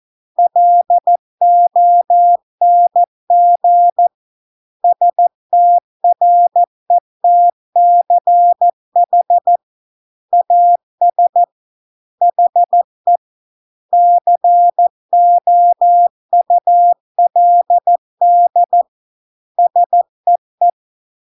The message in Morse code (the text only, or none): LONG STRETCH AS HE COULD SEE